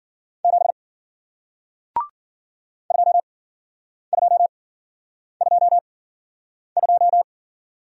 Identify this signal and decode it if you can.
Morse code, heard as a keyed tone, about 53 wpm, spaced as for 9 wpm, 700 Hz. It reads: 7 2222